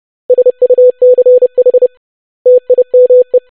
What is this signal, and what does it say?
Morse code, heard as a keyed tone, about 30 wpm, 500 Hz. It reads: SUCH TIME
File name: basic100-25wpm